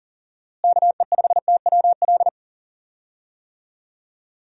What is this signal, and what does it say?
Morse code, heard as a keyed tone, about 40 wpm, 700 Hz. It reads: KE5TWL
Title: USA Call Signs - 6 Characters - Repeated 3x - 40wpm